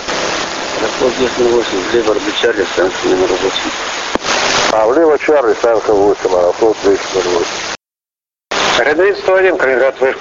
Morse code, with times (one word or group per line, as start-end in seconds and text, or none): none